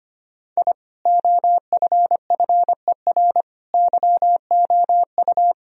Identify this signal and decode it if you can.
Morse code, heard as a keyed tone, about 25 wpm, 700 Hz. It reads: I OFFER YOU
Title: Sentences from Top 1000 Words - Review Part 4 - 25wpm